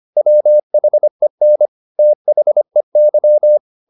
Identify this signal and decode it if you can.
Morse code, heard as a keyed tone, about 25 wpm, 600 Hz. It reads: WHEN THEY